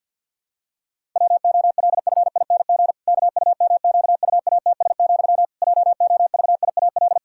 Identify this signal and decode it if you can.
Morse code, heard as a keyed tone, about 53 wpm, 700 Hz. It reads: WOLFING KUMQUATS, JOVIAL
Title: Pangram Challenge v2 - 50wpm